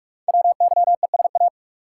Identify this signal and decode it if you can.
Morse code, heard as a keyed tone, about 44 wpm, 690 Hz. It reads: WYESA
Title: Code Groups - 5 Letters - 45wpm